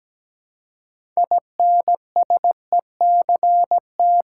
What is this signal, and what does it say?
Morse code, heard as a keyed tone, about 17 wpm, 700 Hz. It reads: INSECT
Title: Six Letter Words - 17wpm